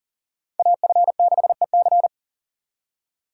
Morse code, heard as a keyed tone, about 51 wpm, 700 Hz. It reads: AF6EC